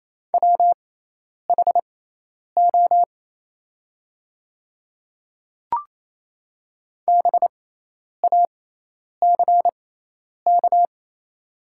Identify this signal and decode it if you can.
Morse code, heard as a keyed tone, about 28 wpm, spaced as for 10 wpm, 700 Hz. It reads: WHO BACK
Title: 6x Character Spacing - Top 300 Words - 28wpm